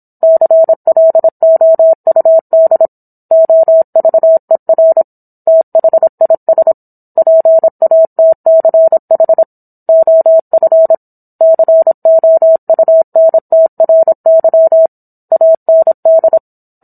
Morse code, heard as a keyed tone, about 27 wpm, 650 Hz. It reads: CLOUD OVER THIS PATCH OF COUNTRY AND